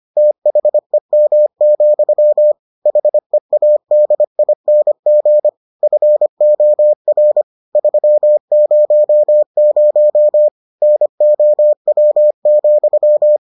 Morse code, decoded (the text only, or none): THEM, HEADING FOR 300 NOW,